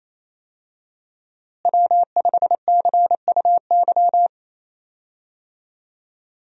Morse code, heard as a keyed tone, about 28 wpm, 700 Hz. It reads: W5CUY
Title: USA Call Signs - 5 Characters - Repeated 2x - 28wpm